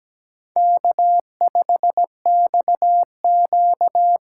Morse code, heard as a keyed tone, about 17 wpm, 700 Hz. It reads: K5XQ
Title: K1USN SST QSO Practice - 17wpm